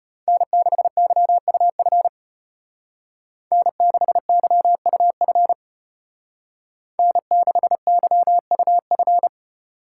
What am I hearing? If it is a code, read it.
Morse code, heard as a keyed tone, about 30 wpm, 700 Hz. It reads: N6YUF N6YUF N6YUF
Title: Speed Racing - USA 1x3 Call Signs 25wpm